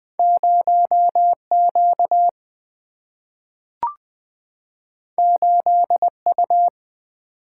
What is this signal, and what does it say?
Morse code, heard as a keyed tone, about 20 wpm, 700 Hz. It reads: 0 Q 8 U
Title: Single-Letter Single-Number 20wpm